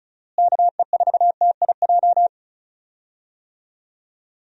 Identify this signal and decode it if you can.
Morse code, heard as a keyed tone, about 36 wpm, 700 Hz. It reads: KE4TIJ
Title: USA Call Signs - 6 Characters - Repeated 2x - 35wpm